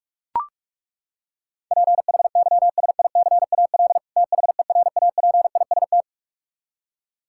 Morse code, heard as a keyed tone, about 41 wpm, 700 Hz. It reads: PHYSICAL THERAPIST
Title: Practice QSO - Exchange 3 - More Info - 45wpm